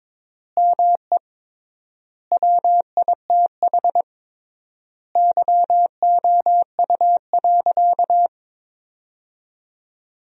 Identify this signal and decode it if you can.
Morse code, heard as a keyed tone, about 22 wpm, 700 Hz. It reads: ME WITH YOU.
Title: 3x Word-Spacing - Sentences from Top 200 Words - 22wpm